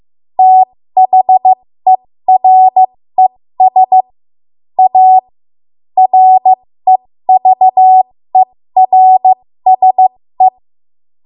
Morse code, heard as a keyed tone, about 15 wpm, spaced as for 13 wpm, 750 Hz. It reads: THERES A REVERSE
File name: Morse / 13WPM